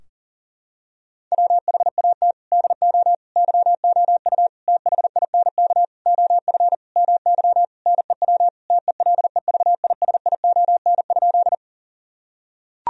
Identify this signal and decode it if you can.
Morse code, heard as a keyed tone, about 37 wpm, 700 Hz. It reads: WHAT DO YOU THINK OF MY NEW TELEVISION?
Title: Sentences from Top 1000 Words - Review Part 4 - 40wpm